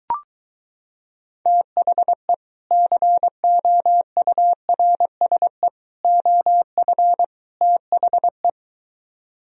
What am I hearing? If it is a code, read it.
Morse code, heard as a keyed tone, about 23 wpm, 700 Hz. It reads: THE COURSE OF THE